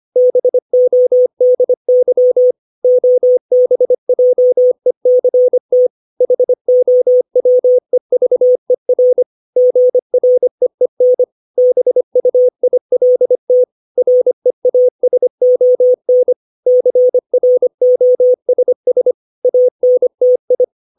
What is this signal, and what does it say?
Morse code, heard as a keyed tone, about 25 wpm, 500 Hz. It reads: BODY OBJECT HOWEVER GREEN BUILT REASON CROSS ANTI
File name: 632 words / 632 Words 25WPM